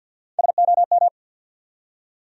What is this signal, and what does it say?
Morse code, heard as a keyed tone, about 53 wpm, 700 Hz. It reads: SOM
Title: USA Call Sign Postfixes - 3 Letters 50wpm